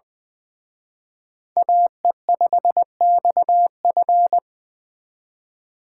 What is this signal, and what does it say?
Morse code, heard as a keyed tone, about 24 wpm, 700 Hz. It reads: AE5XF
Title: Speed Racing - USA Call Signs 20wpm